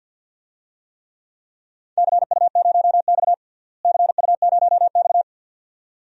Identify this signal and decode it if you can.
Morse code, heard as a keyed tone, about 51 wpm, 700 Hz. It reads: CU0X CU0X
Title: DXCC-100 Call Signs - 4 Characters - Repeated 2x - 50wpm